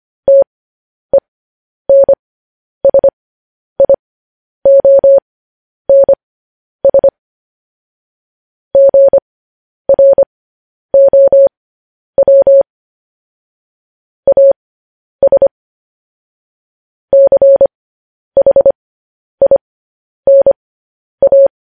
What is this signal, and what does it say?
Morse code, heard as a keyed tone, about 25 wpm, spaced as for 10 wpm, 550 Hz. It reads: TENSIONS GROW AS CHINA
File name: podcast / News headlines-1x-25x10